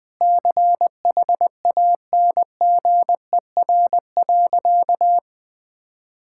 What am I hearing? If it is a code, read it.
Morse code, heard as a keyed tone, about 19 wpm, 700 Hz. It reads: CHANGER.
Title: Sentences with 6-Word Phrases - 20wpm